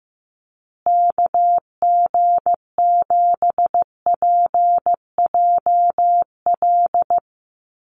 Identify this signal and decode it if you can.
Morse code, heard as a keyed tone, about 15 wpm, 700 Hz. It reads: KG7PJL
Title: USA Call Signs Repeated 3x 15wpm